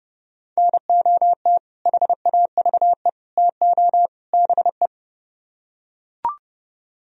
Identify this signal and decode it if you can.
Morse code, heard as a keyed tone, about 28 wpm, 700 Hz. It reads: NOT HAVE TO BE